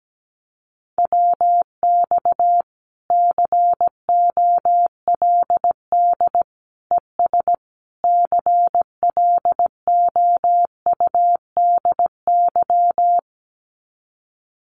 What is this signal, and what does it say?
Morse code, heard as a keyed tone, about 17 wpm, 700 Hz. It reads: WX COLD ES CLOUDY
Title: Common QSO Phrases Encore 17wpm